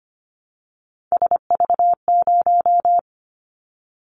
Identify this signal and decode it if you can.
Morse code, heard as a keyed tone, about 25 wpm, 700 Hz. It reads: SV0
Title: Callsign Prefixes for Top 50 DXCC - 25wpm